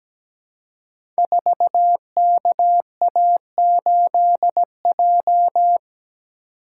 Morse code, heard as a keyed tone, about 17 wpm, 700 Hz. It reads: 4KA8J
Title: Code Groups - 5 Letter:Numbers - 17wpm